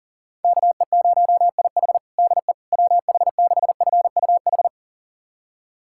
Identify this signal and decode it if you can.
Morse code, heard as a keyed tone, about 40 wpm, 700 Hz. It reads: KE0IH DE WH6FUH
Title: USA Callsign DE Callsign 40wpm